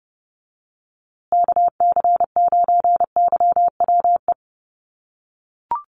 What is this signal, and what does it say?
Morse code, heard as a keyed tone, about 30 wpm, 700 Hz. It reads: KC9YWE
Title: USA Call Signs Repeated 3x 30wpm